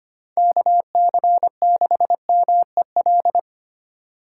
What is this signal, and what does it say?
Morse code, heard as a keyed tone, about 25 wpm, 700 Hz. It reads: KC6MEL
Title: USA Call Signs - 6 Characters - 3x Character Spacing - 25wpm